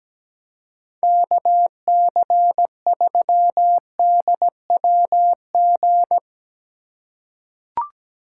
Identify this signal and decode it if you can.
Morse code, heard as a keyed tone, about 17 wpm, 700 Hz. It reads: KC3DWG
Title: USA Call Signs - 6 Characters - 4x Character Spacing - 17wpm